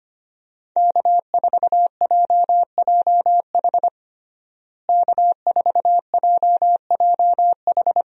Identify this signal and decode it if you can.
Morse code, heard as a keyed tone, about 25 wpm, 700 Hz. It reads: K4JJH K4JJH
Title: USA Call Signs - 5 Characters - Repeated 2x - 25wpm